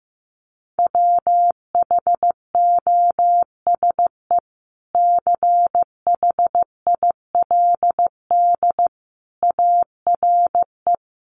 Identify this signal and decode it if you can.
Morse code, heard as a keyed tone, about 15 wpm, 700 Hz. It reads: WHOSE CHILD ARE
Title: Sentences from Top 1200 Words 15wpm